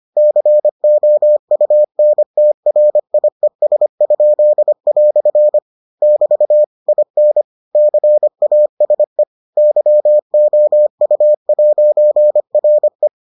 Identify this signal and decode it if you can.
Morse code, heard as a keyed tone, about 25 wpm, 600 Hz. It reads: COUNTRIES?" <BT> IN CASE YOU'RE